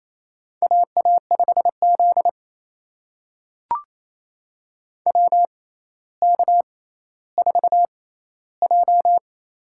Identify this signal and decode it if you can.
Morse code, heard as a keyed tone, about 28 wpm, spaced as for 14 wpm, 700 Hz. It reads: AA5Z WK4J
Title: USA Call Signs - 4 Characters - 6x Character Spacing - 28wpm